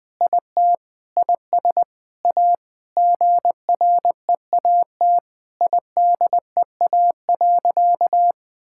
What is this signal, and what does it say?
Morse code, heard as a keyed tone, about 20 wpm, 700 Hz. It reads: IT IS A GREAT IDEA.
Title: Sentences with 7-Word Phrases - 20wpm